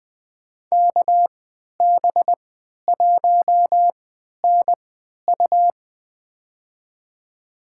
Morse code, heard as a keyed tone, about 20 wpm, spaced as for 11 wpm, 700 Hz. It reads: KB1NU
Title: USA Call Signs - 5 Characters - 3x Character Spacing - 20wpm